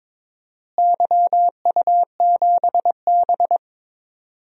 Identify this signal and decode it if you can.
Morse code, heard as a keyed tone, about 22 wpm, 700 Hz. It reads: YU7B
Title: DXCC-100 Call Signs - 4 Characters - Repeated 2x - 22wpm